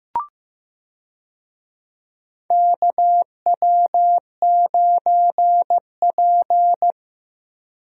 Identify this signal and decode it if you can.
Morse code, heard as a keyed tone, about 16 wpm, 700 Hz. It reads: KW9P
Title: USA Call Signs - 4 Characters - Repeated 3x - 15wpm